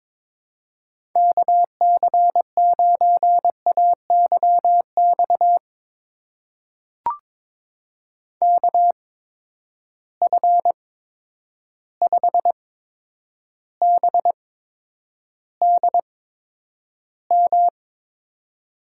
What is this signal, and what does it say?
Morse code, heard as a keyed tone, about 22 wpm, spaced as for 10 wpm, 700 Hz. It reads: KC9AYX KF5BDM
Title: USA Call Signs - 6 Characters - 8x Character Spacing - 22wpm